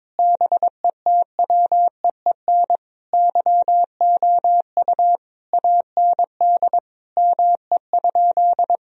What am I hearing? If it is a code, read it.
Morse code, heard as a keyed tone, about 22 wpm, 700 Hz. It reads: BETWEEN YOU AND ME?
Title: Sentences from Top 1000 Words - Review Part 3 - 22wpm